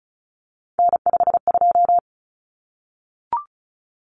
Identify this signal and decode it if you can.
Morse code, heard as a keyed tone, about 34 wpm, 700 Hz. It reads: N52
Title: Common Cut Numbers - 3-Digit 35wpm